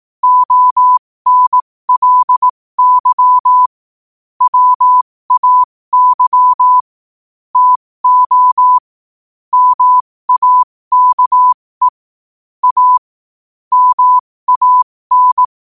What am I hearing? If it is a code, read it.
Morse code, heard as a keyed tone, about 18 wpm, spaced as for 15 wpm, 1000 Hz. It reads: ONLY WAY TO MAKE A MAN
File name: qotdfiles / 15wpm